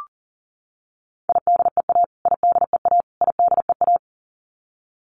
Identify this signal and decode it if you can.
Morse code, heard as a keyed tone, about 40 wpm, 700 Hz. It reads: IDEA IDEA IDEA
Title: Top 500 Words Repeated 40wpm